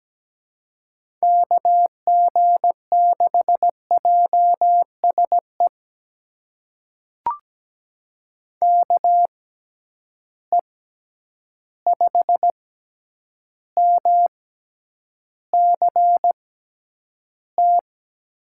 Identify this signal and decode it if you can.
Morse code, heard as a keyed tone, about 17 wpm, spaced as for 9 wpm, 700 Hz. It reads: KG6JSE KE5MCT
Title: USA Call Signs - 6 Characters - 6x Character Spacing - 17wpm